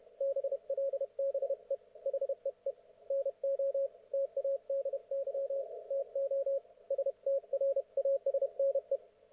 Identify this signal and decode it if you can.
Morse code, heard as a keyed tone, about 31 wpm, 530 Hz. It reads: BLBE HEE NO TADY TO STRASNE
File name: Archiv / CUC107 / ok1fhi cul107